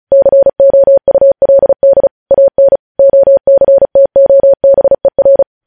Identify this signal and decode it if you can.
Morse code, heard as a keyed tone, about 36 wpm, 550 Hz. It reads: COULD AN OCTOBER